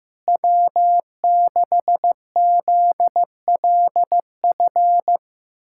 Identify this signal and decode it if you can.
Morse code, heard as a keyed tone, about 16 wpm, 700 Hz. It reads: W6ZLF
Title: USA Callsign DE Callsign - 5 Characters - 15wpm